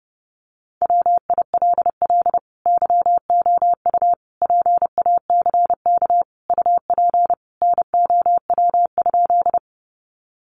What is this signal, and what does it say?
Morse code, heard as a keyed tone, about 28 wpm, 700 Hz. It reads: WILL YOU PACK UP NOW?
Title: Sentences from Top 1300 Words 30wpm